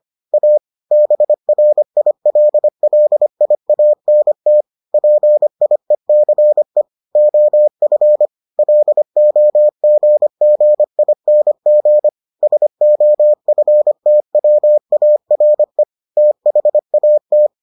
Morse code, heard as a keyed tone, about 25 wpm, 600 Hz. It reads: A BRILLIANT PIECE OF LOGGING SOFTWARE THAT